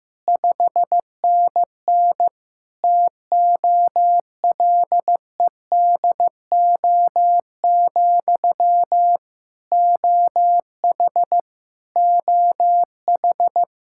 Morse code, heard as a keyed tone, about 15 wpm, 700 Hz. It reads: 5NN TOLEDO, OH OH
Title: Practice QSO - Exchange 1 - Beginning - 15wpm